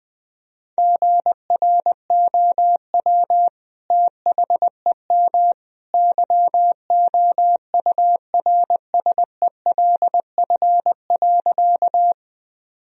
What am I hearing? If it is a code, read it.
Morse code, heard as a keyed tone, about 20 wpm, 700 Hz. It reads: GROW THEM YOURSELF.
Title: Sentences from Top 1000 Words - Review Part 1 - 20wpm